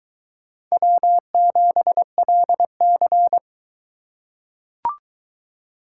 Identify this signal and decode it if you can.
Morse code, heard as a keyed tone, about 22 wpm, 700 Hz. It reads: W7LC